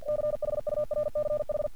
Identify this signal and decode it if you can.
Morse code, heard as a keyed tone, about 48 wpm, 600 Hz. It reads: CHURCH
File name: archived / CRSnet